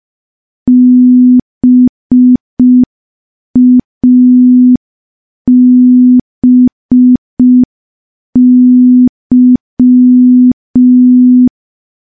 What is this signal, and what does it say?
Morse code, heard as a keyed tone, about 5 wpm, 250 Hz. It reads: BABY
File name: morse / morse-training-05WPM